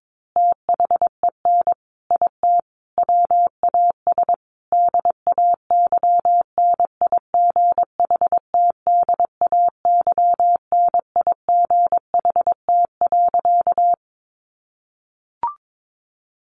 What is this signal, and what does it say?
Morse code, heard as a keyed tone, about 22 wpm, 700 Hz. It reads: THEN IT WAS DAY NIGHT DAY NIGHT.
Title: A Sound Of Thunder 22wpm